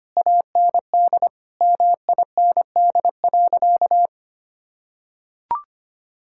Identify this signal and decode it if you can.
Morse code, heard as a keyed tone, about 25 wpm, 700 Hz. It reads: AND MIND.
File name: practice / 25wpm / Difficult Sentences with Hints - 25wpm